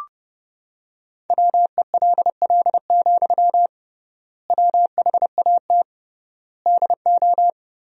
Morse code, heard as a keyed tone, about 32 wpm, 700 Hz. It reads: WELL, WHAT DO
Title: 3x Word-Spacing - Sentences from Top 200 Words - 30wpm